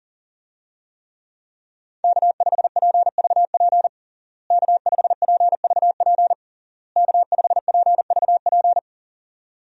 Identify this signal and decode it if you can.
Morse code, heard as a keyed tone, about 40 wpm, 700 Hz. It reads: K5PVP K5PVP K5PVP
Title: USA Call Signs - 5 Characters - Repeated 3x - 40wpm